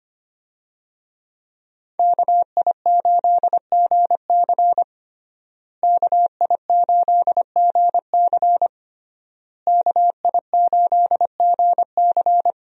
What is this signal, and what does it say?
Morse code, heard as a keyed tone, about 25 wpm, 700 Hz. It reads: KI8GC KI8GC KI8GC
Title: USA Call Signs - 5 Characters - Repeated 3x - 25wpm